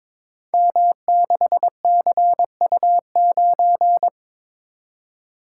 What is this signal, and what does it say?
Morse code, heard as a keyed tone, about 22 wpm, 700 Hz. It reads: M6CU9
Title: Code Groups - 5 Letter:Numbers - 22wpm